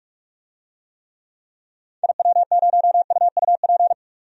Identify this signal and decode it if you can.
Morse code, heard as a keyed tone, about 43 wpm, 700 Hz. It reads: IW0UUP
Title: DXCC-100 Call Signs - 6 Characters - 45wpm